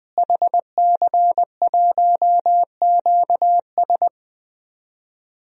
Morse code, heard as a keyed tone, about 21 wpm, 700 Hz. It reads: HC1QS
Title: DXCC-100 Call Signs - 5 Characters - Repeated 3x - 20wpm